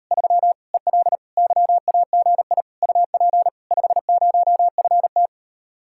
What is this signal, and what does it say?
Morse code, heard as a keyed tone, about 38 wpm, 700 Hz. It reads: 3 EL YAGI UP 50FT